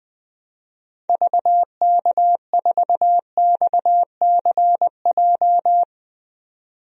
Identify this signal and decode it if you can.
Morse code, heard as a keyed tone, about 20 wpm, 700 Hz. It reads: VK4XCJ
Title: DXCC-100 Call Signs - 6 Characters - 20wpm